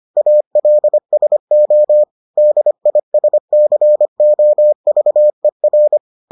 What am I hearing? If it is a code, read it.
Morse code, heard as a keyed tone, about 25 wpm, 600 Hz. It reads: ALSO DISCOVER